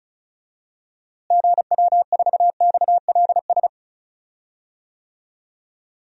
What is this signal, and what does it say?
Morse code, heard as a keyed tone, about 36 wpm, 700 Hz. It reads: GW4XLS
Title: DXCC-100 Call Signs - 6 Characters - Repeated 3x - 35wpm